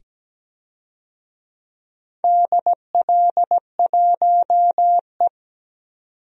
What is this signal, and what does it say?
Morse code, heard as a keyed tone, about 17 wpm, 700 Hz. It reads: DL1E
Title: DXCC-100 Call Signs - 4 Characters - 17wpm